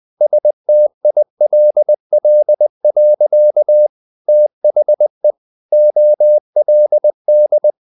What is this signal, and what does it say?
Morse code, heard as a keyed tone, about 20 wpm, 600 Hz. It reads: STILL. THE OLD
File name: the old man and the sea part 2 20wpm